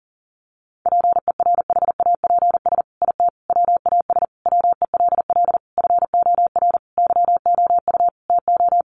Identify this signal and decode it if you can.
Morse code, heard as a keyed tone, about 40 wpm, 700 Hz. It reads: PERHAPS IT WAS WELL FOR YOU TO
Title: Sentences from Top 1400 Words 40wpm